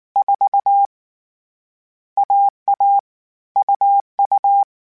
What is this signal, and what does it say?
Morse code, heard as a keyed tone, about 19 wpm, 800 Hz. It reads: 4 AA UU
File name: Difficult Combos / Difficult Combos-mp3s / AUV AUV4 @18